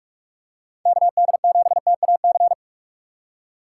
Morse code, heard as a keyed tone, about 47 wpm, 700 Hz. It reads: KD7TAC
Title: USA Call Signs - 6 Characters - 2x Character Spacing - 45wpm